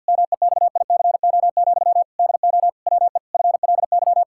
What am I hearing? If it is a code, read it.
Morse code, heard as a keyed tone, about 52 wpm, 700 Hz. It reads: MEXICO, DO WE FLY
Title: Pangram Challenge v2 - 50wpm